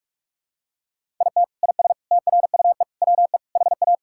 Morse code, heard as a keyed tone, about 47 wpm, 700 Hz. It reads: IT IS TRUE WE HA
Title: Sentences from Top 1000 Words - Review Part 5 - 45wpm